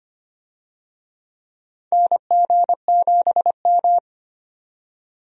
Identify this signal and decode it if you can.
Morse code, heard as a keyed tone, about 25 wpm, 700 Hz. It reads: NG7M
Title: USA Call Signs - 4 Characters - 25wpm